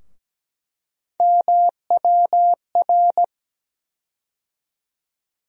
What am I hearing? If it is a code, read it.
Morse code, heard as a keyed tone, about 17 wpm, 700 Hz. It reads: MWR
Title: USA Call Sign Postfixes - 3 Letters 17wpm